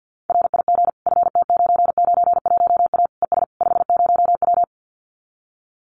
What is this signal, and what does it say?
Morse code, heard as a keyed tone, about 50 wpm, 700 Hz. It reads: RIG FT991A ES 50W
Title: Common QSO Phrases Encore 50wpm